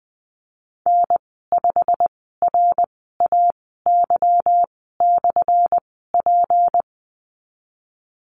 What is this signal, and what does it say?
Morse code, heard as a keyed tone, about 20 wpm, spaced as for 15 wpm, 700 Hz. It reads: N5RAY/P
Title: 179 Lesson 29g - Slash - Call Signs 20wpm